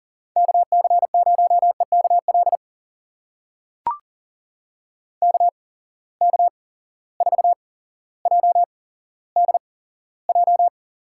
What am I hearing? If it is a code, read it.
Morse code, heard as a keyed tone, about 40 wpm, spaced as for 17 wpm, 700 Hz. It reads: KC0EKL KK4JDJ
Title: USA Call Signs - 6 Characters - 8x Character Spacing - 40wpm